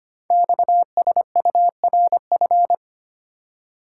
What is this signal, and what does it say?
Morse code, heard as a keyed tone, about 25 wpm, 700 Hz. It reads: XSURF
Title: Code Groups - 5 Letters - 25wpm